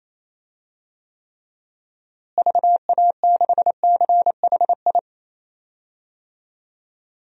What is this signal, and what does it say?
Morse code, heard as a keyed tone, about 28 wpm, 700 Hz. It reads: VA6CHI
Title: DXCC-100 Call Signs - 6 Characters - Repeated 2x - 28wpm